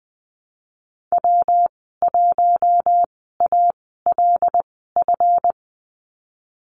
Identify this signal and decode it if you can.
Morse code, heard as a keyed tone, about 20 wpm, spaced as for 15 wpm, 700 Hz. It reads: W1ALF
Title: 186 Lesson 30g - Letter b - Call Signs 20wpm